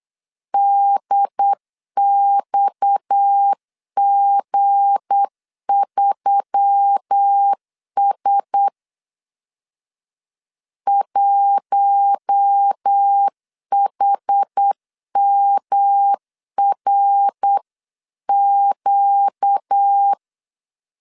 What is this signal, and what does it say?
Morse code, heard as a keyed tone, about 8 wpm, 800 Hz. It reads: DXG3S 1HMRQ
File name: CW / DET / 1 9 1500C S C 8wpm 5653